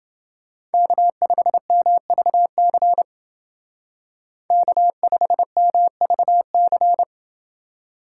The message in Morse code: K5MVC K5MVC

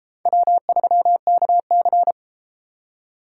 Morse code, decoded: W3KC

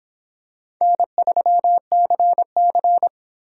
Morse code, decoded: N3CC

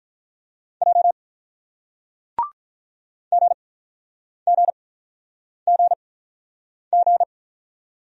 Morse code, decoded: J GGGG

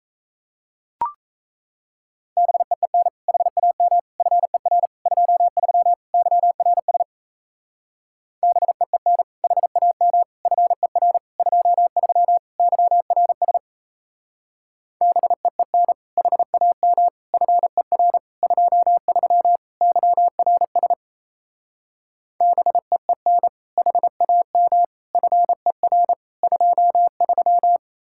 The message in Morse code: BEEN HAM FER 23 YRS BEEN HAM FER 23 YRS BEEN HAM FER 23 YRS BEEN HAM FER 23

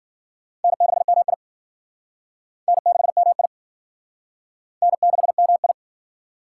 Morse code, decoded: N6MI N6MI N6MI

